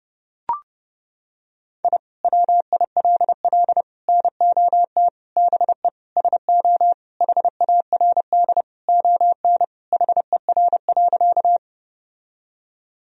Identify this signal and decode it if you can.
Morse code, heard as a keyed tone, about 30 wpm, 700 Hz. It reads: I WILL NOT BE SO HARD ON HER.
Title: Sentences from Top 1000 Words - Review Part 1 - 30wpm